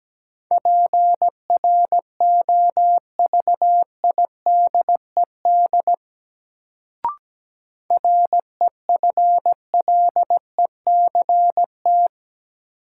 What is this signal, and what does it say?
Morse code, heard as a keyed tone, about 17 wpm, 700 Hz. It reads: PROVIDED REFLECT
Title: Sentences from Top 1500 Words - New Word Drill - 17wpm